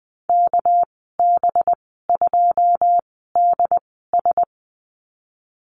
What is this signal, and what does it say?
Morse code, heard as a keyed tone, about 20 wpm, spaced as for 14 wpm, 700 Hz. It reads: KB2DS
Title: 207 Lesson 33g - Letter j - Call Signs 20wpm